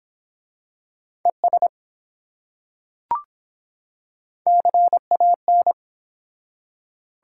Morse code, heard as a keyed tone, about 23 wpm, 700 Hz. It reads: ES CAN